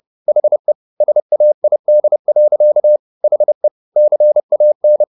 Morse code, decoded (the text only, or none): HE SAID. HE CAN